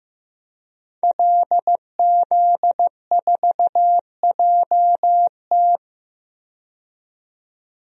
Morse code, heard as a keyed tone, about 15 wpm, 700 Hz. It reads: LZ4JT
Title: DXCC-100 Call Signs - 5 Characters - 15wpm